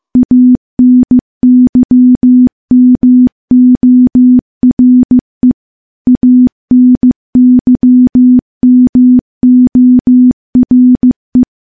Morse code, heard as a keyed tone, about 15 wpm, 250 Hz. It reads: ANYMORE ANYMORE
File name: morse-training-15WPM